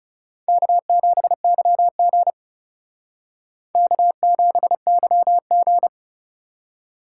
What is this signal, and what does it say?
Morse code, heard as a keyed tone, about 39 wpm, 700 Hz. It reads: K7YG K7YG